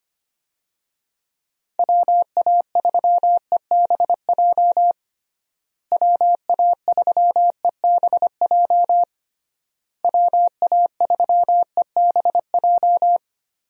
Morse code, heard as a keyed tone, about 25 wpm, 700 Hz. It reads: WA3EBJ WA3EBJ WA3EBJ
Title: USA Call Signs - 6 Characters - Repeated 3x - 25wpm